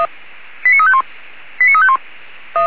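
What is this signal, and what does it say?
Five-tone selective call (ZVEI), heard as quick runs of five tones, tones of about 70 ms. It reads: 89351 89351
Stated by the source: ZVEI five-tone selective call received by SDR.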